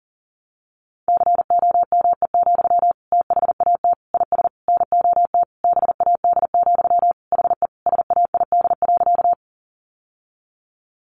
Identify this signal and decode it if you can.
Morse code, heard as a keyed tone, about 40 wpm, 700 Hz. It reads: COME, THAT IS NOT BAD, HE SAID.
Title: Sentences from Top 700 Words 40wpm